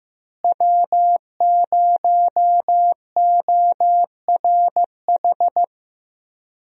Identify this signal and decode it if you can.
Morse code, heard as a keyed tone, about 16 wpm, 700 Hz. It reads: W0ORH
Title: USA Call Signs - 5 Characters - Repeated 3x - 15wpm